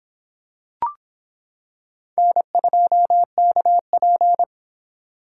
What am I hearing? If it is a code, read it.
Morse code, heard as a keyed tone, about 26 wpm, 700 Hz. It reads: N2KP